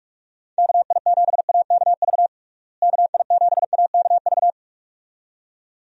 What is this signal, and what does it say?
Morse code, heard as a keyed tone, about 46 wpm, 700 Hz. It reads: KI7AKV KI7AKV
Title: USA Call Signs - 6 Characters - Repeated 2x - 45wpm